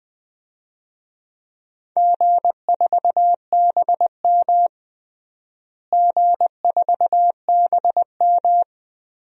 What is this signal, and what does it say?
Morse code, heard as a keyed tone, about 21 wpm, 700 Hz. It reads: G4BM G4BM
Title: DXCC-100 Call Signs - 4 Characters - Repeated 2x - 20wpm